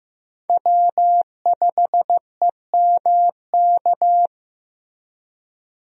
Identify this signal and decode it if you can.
Morse code, heard as a keyed tone, about 16 wpm, 700 Hz. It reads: W5EMK
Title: USA Call Signs - 5 Characters - 15wpm